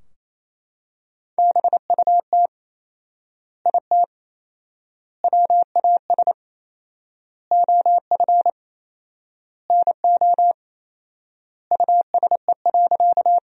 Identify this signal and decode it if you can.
Morse code, heard as a keyed tone, about 28 wpm, 700 Hz. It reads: BUT IT WAS OF NO USE.
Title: 4x Word-Spacing - Sentences from Top 100 Words - 28wpm